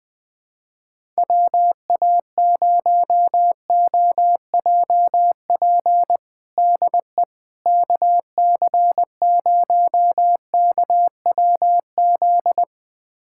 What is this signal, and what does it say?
Morse code, heard as a keyed tone, about 21 wpm, 700 Hz. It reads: WA0OJP DE KC0KWZ
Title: USA Callsign DE Callsign - 6 Characters - 20wpm